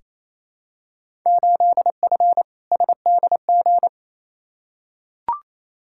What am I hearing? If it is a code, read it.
Morse code, heard as a keyed tone, about 28 wpm, 700 Hz. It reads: 8F SDG
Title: Field Day Exchanges Encore - 28wpm